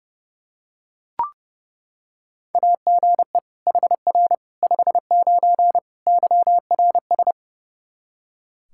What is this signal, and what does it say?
Morse code, heard as a keyed tone, about 30 wpm, 700 Hz. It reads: AGE HR 59 YRS
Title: K1USN SST QSO Practice 30wpm